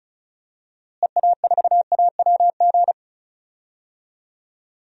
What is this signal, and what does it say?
Morse code, heard as a keyed tone, about 34 wpm, 700 Hz. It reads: EA4AWG
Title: DXCC-100 Call Signs - 6 Characters - Repeated 3x - 35wpm